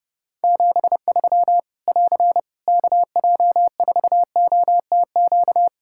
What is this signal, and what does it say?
Morse code, heard as a keyed tone, about 30 wpm, 700 Hz. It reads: 73 <AR> KJ4OTQ
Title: Practice QSO - Exchange 4 - Ending - 30wpm